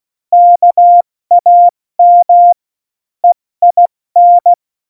Morse code, heard as a keyed tone, about 16 wpm, spaced as for 14 wpm, 700 Hz. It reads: KAM EIN
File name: Morsekurs / Witze / Witz 01. Oktober 2019, 70 F80